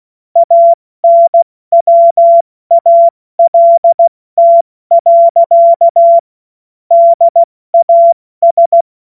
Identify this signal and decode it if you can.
Morse code, heard as a keyed tone, about 16 wpm, 680 Hz. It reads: ANWALT. DAS
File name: Morsekurs / Witze / Witz 08. November 2022 70 F80